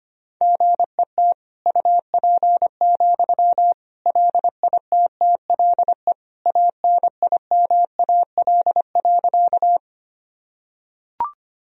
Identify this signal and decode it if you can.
Morse code, heard as a keyed tone, about 25 wpm, 700 Hz. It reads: GET UP, LITTLE ANIMAL.
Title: Sentences from Top 1000 Words - Review Part 1 - 25wpm